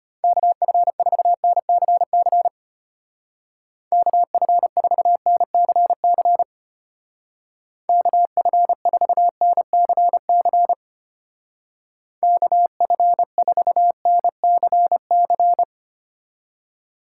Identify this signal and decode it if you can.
Morse code, heard as a keyed tone, about 31 wpm, 700 Hz. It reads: KF4NCC KF4NCC KF4NCC KF4NCC
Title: Speed Racing - USA 2x3 Call Signs 25wpm